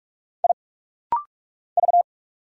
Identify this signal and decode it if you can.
Morse code, heard as a keyed tone, about 46 wpm, 700 Hz. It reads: I V
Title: Single Letter-Number Alt - Rapid-Fire 45wpm